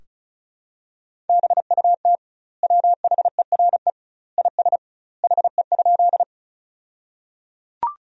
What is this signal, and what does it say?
Morse code, heard as a keyed tone, about 34 wpm, 700 Hz. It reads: BUT WHERE IS HE?
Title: 2x Word-Spacing - Sentences from Top 200 Words - 35wpm